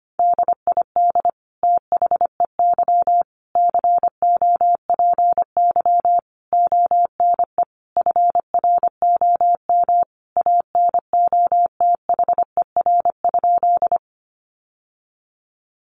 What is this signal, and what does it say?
Morse code, heard as a keyed tone, about 25 wpm, 700 Hz. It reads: DID THEY COPY ONE FROM ANOTHER?
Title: Sentences from Top 1400 Words 25wpm